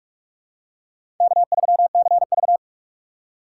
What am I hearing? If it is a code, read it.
Morse code, heard as a keyed tone, about 45 wpm, 700 Hz. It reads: K3CV
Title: USA Call Signs - 4 Characters - 2x Character Spacing - 45wpm